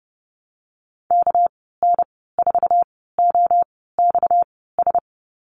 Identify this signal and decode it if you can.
Morse code, heard as a keyed tone, about 30 wpm, spaced as for 17 wpm, 700 Hz. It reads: KN4OXS
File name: Lesson 37g - Letter x - Call Signs 30wpm